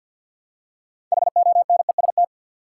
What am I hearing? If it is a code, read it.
Morse code, heard as a keyed tone, about 49 wpm, 700 Hz. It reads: HONEST
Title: Six Letter Words - 50wpm